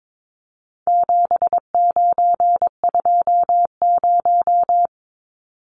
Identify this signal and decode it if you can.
Morse code, heard as a keyed tone, about 22 wpm, 700 Hz. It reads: 7920
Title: Numbers - 4-Digit 22wpm